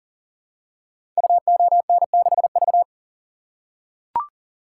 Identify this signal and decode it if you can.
Morse code, heard as a keyed tone, about 42 wpm, 700 Hz. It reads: UON6V
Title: Code Groups - 5 Letter:Numbers - 40wpm